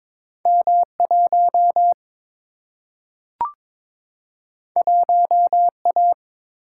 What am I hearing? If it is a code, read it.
Morse code, heard as a keyed tone, about 22 wpm, 700 Hz. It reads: M1 1A
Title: Single-Letter Single-Number 22wpm